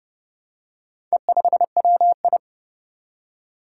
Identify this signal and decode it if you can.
Morse code, heard as a keyed tone, about 30 wpm, 700 Hz. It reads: E5WI
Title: Four Challenging Letter-Number Combinations 30wpm